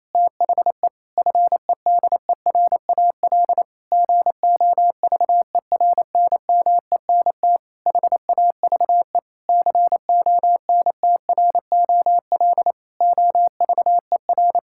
Morse code, heard as a keyed tone, about 28 wpm, 700 Hz. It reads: THE FEDERAL GOVERNMENT HAVE CONTROL OVER
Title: Sentences from Top 1000 Words - Review Part 5 - 28wpm